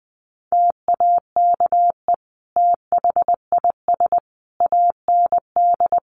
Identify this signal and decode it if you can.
Morse code, heard as a keyed tone, about 20 wpm, 700 Hz. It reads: TAKE THIS AND
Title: Sentences from Top 200 Words 20wpm